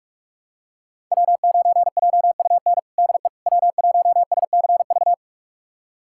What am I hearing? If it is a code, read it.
Morse code, heard as a keyed tone, about 42 wpm, 700 Hz. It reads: W9JUN DE W1SCV
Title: USA Callsign DE Callsign - 5 Characters - 45wpm